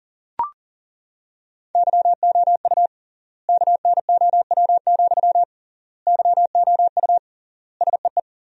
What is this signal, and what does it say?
Morse code, heard as a keyed tone, about 40 wpm, 700 Hz. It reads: YOU KNOW, YOU SEE
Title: 3x Word-Spacing - Sentences from Top 200 Words - 40wpm